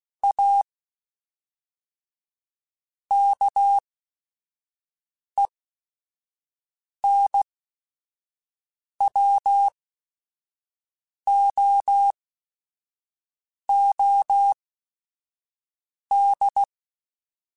Morse code, heard as a keyed tone, about 16 wpm, spaced as for 5 wpm, 800 Hz. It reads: A KENWOOD
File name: QSOs / QSOs-mp3s / kc7sac@5